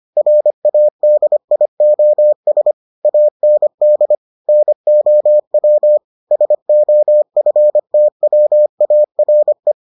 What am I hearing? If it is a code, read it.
Morse code, heard as a keyed tone, about 25 wpm, 600 Hz. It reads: RADIOS AND NOW SOFTWARE